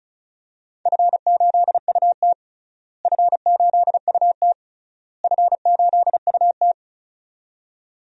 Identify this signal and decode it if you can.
Morse code, heard as a keyed tone, about 36 wpm, 700 Hz. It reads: F8UT F8UT F8UT
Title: DXCC-100 Call Signs - 4 Characters - Repeated 3x - 35wpm